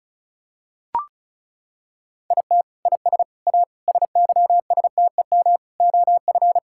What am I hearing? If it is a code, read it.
Morse code, heard as a keyed tone, about 36 wpm, 700 Hz. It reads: IT IS A SYSTEM OF
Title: Sentences from Top 1000 Words - Review Part 6 - 35wpm